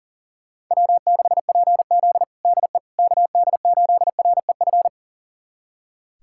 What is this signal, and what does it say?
Morse code, heard as a keyed tone, about 40 wpm, 700 Hz. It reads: W6PZ DE KD8REF
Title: USA Callsign DE Callsign 40wpm